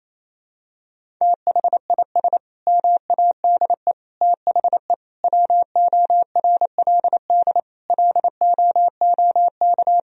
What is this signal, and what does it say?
Morse code, heard as a keyed tone, about 28 wpm, 700 Hz. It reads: THIS MADE THE WORLD LOOK
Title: Sentences from Top 1000 Words - Review Part 2 - 28wpm